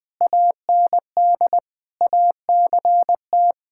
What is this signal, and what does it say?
Morse code, heard as a keyed tone, about 20 wpm, 700 Hz. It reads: AND ACT
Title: All-Age Appropriate Jokes - 20wpm